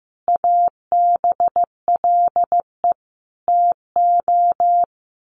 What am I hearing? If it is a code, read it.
Morse code, heard as a keyed tone, about 15 wpm, 700 Hz. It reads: ABLE TO
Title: Sets of 5 Words Encore2 15wpm